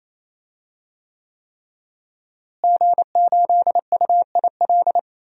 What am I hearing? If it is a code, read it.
Morse code, heard as a keyed tone, about 27 wpm, 700 Hz. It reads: G8UIL
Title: DXCC-100 Call Signs - 5 Characters - Repeated 3x - 28wpm